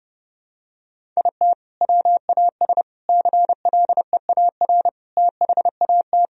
Morse code, heard as a keyed tone, about 30 wpm, 700 Hz. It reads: IT WAS CLEAR THAT
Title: Sentences from Top 1000 Words - Review Part 4 - 30wpm